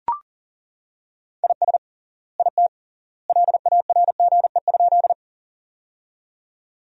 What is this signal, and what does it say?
Morse code, heard as a keyed tone, about 37 wpm, 700 Hz. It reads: IS IT LARGE?
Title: 3x Word-Spacing - Sentences from Top 200 Words - 40wpm